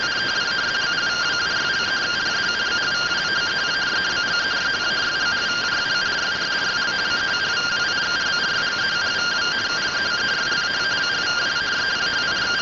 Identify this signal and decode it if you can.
Two-tone teleprinter signal, about 170 Hz shift.